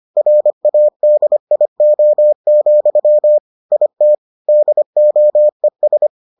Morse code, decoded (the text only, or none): RADIO, IT DOES